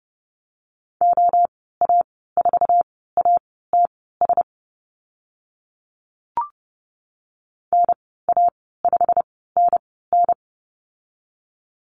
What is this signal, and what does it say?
Morse code, heard as a keyed tone, about 30 wpm, spaced as for 17 wpm, 700 Hz. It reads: OA4ATS NA5NN
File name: Lesson 15g - Number 5 - Call Signs 30wpm